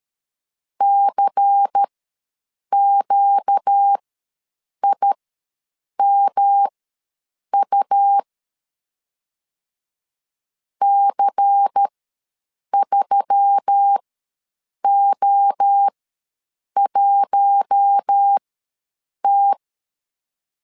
Morse code, heard as a keyed tone, about 13 wpm, spaced as for 7 wpm, 800 Hz. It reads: CQIMU C3O1T
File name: LEZIONI / 5 LBVC / 5 LBVC ws 12wpm 3159